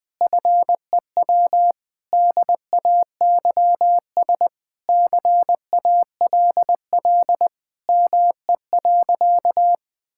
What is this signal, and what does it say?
Morse code, heard as a keyed tone, about 20 wpm, 700 Hz. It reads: FEW DAYS CALL ME.
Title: Sentences with 3-Word Phrases - 20wpm